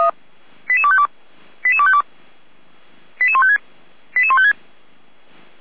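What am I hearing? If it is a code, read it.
Five-tone selective call (ZVEI), heard as quick runs of five tones, tones of about 70 ms. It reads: 88252 88252 88156 88156
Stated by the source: ZVEI five-tone call received on an SDR.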